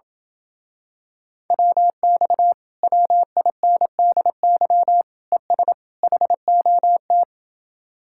Morse code, heard as a keyed tone, about 28 wpm, 700 Hz. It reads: WX WINDY ES HOT